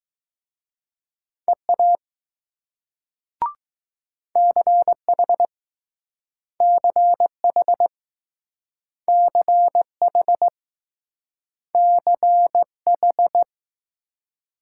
Morse code, heard as a keyed tone, about 20 wpm, 700 Hz. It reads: EA CH CH CH CH